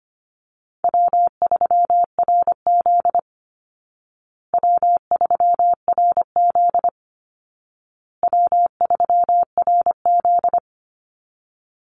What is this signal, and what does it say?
Morse code, heard as a keyed tone, about 25 wpm, 700 Hz. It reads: W3RZ W3RZ W3RZ
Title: Easy Call Signs Repeated 3x - 25wpm